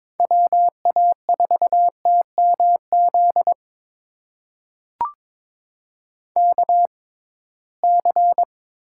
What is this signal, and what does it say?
Morse code, heard as a keyed tone, about 22 wpm, spaced as for 17 wpm, 700 Hz. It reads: WA4TMZ KC
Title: USA Call Signs - 6 Characters - 6x Character Spacing - 22wpm